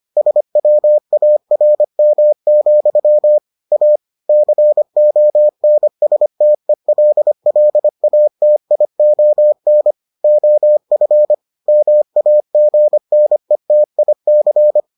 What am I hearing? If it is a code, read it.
Morse code, heard as a keyed tone, about 25 wpm, 600 Hz. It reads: SWARM, A CONSTELLATION OF MAGNETIC